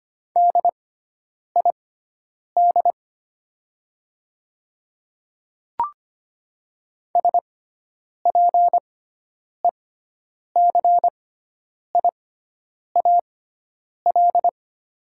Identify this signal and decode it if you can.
Morse code, heard as a keyed tone, about 25 wpm, spaced as for 9 wpm, 690 Hz. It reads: DID SPECIAL
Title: 6x Character Spacing - Top 500 Words - 25wpm